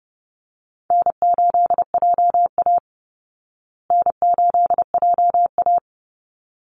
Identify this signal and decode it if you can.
Morse code, heard as a keyed tone, about 30 wpm, 700 Hz. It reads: N8JA N8JA
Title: Easy Call Signs Repeated 2x - 30wpm